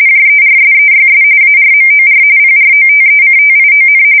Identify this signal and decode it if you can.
RTTY, about 45.45 baud. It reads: JERSEYS ARMOR PROTECTION
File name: fsk170 / USS-New-Jersey